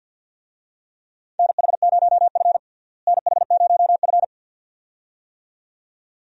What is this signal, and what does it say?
Morse code, heard as a keyed tone, about 52 wpm, 700 Hz. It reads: NH0F NH0F
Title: USA Call Signs - 4 Characters - Repeated 2x - 50wpm